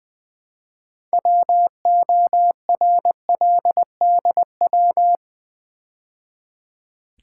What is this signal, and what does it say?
Morse code, heard as a keyed tone, about 20 wpm, 700 Hz. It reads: WORLDW
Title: Build A Word - 3 Letters At A Time - 9-Letter Words - 20wpm